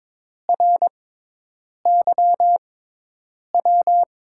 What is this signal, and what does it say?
Morse code, heard as a keyed tone, about 22 wpm, spaced as for 8 wpm, 700 Hz. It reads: RYW
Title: USA Call Signs - 6x Character Spacing - 22wpm